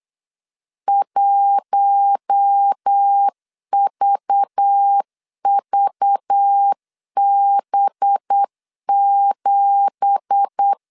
Morse code, heard as a keyed tone, about 8 wpm, 800 Hz. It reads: 1VVB7
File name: CW / DET / 1 9 1500C S C 8wpm 5653